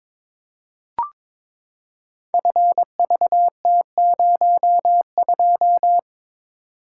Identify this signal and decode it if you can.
Morse code, heard as a keyed tone, about 23 wpm, 700 Hz. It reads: FVT02
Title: Code Groups - 5 Letter:Numbers - 22wpm